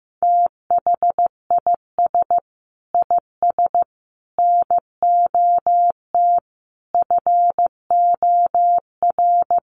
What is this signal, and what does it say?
Morse code, heard as a keyed tone, about 15 wpm, 700 Hz. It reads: THIS IS NOT FOR
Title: Sentences from Top 1400 Words 15wpm